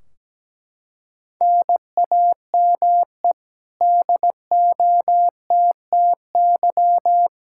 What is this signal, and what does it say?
Morse code, heard as a keyed tone, about 17 wpm, 700 Hz. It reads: NAME DOTTY
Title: Practice QSO - Exchange 1 - Beginning - 17wpm